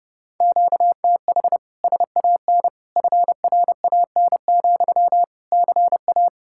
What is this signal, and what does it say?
Morse code, heard as a keyed tone, about 39 wpm, 700 Hz. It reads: QTH SAN FRAN, CA